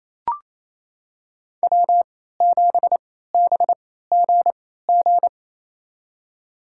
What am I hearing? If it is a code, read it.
Morse code, heard as a keyed tone, about 28 wpm, spaced as for 16 wpm, 700 Hz. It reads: W7BGG
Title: USA Call Signs - 3x Character Spacing - 28wpm